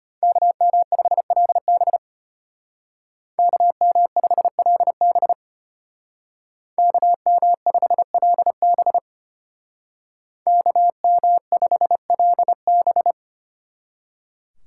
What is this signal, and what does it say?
Morse code, heard as a keyed tone, about 30 wpm, 700 Hz. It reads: KM5LB KM5LB KM5LB KM5LB
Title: Speed Racing - USA 2x2 Call Signs 25wpm